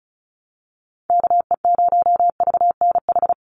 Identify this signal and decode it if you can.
Morse code, heard as a keyed tone, about 34 wpm, 700 Hz. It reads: KE0VNH
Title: USA Call Signs 35wpm